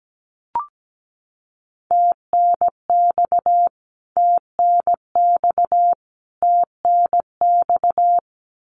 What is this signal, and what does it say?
Morse code, heard as a keyed tone, about 17 wpm, 700 Hz. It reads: TNX TNX TNX
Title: Abbreviations Repeated 3x - 17wpm